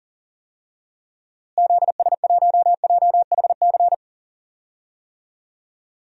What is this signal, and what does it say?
Morse code, heard as a keyed tone, about 40 wpm, 700 Hz. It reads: ZS1JHC
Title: DXCC-100 Call Signs - 6 Characters - Repeated 3x - 40wpm